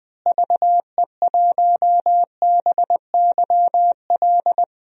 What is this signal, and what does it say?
Morse code, heard as a keyed tone, about 20 wpm, 700 Hz. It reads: VE1BYL
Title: DXCC-100 Call Signs - 6 Characters - Repeated 3x - 20wpm